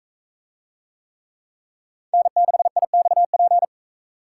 Morse code, heard as a keyed tone, about 42 wpm, 700 Hz. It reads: N6IXP